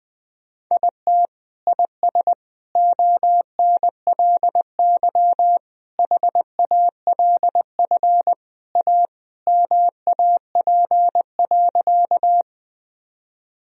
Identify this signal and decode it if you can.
Morse code, heard as a keyed tone, about 20 wpm, 700 Hz. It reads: IT IS ONLY HALF A MAP.
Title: Sentences from Top 1000 Words - Review Part 2 - 20wpm